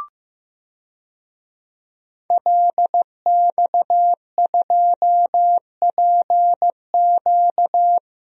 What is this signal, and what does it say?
Morse code, heard as a keyed tone, about 15 wpm, 700 Hz. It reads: LX2PQ
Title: DXCC-100 Call Signs - 5 Characters - 15wpm